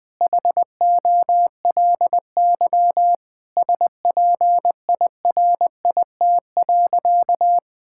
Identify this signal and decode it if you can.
Morse code, heard as a keyed tone, about 20 wpm, 700 Hz. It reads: HOLY SPIRIT.
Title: Sentences with 9-Word Phrases - 20wpm